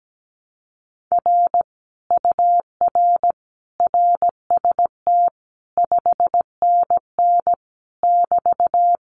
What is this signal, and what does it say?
Morse code, heard as a keyed tone, about 17 wpm, 700 Hz. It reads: R UR RST 5NN <BT>
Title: Common QSO Phrases Encore 17wpm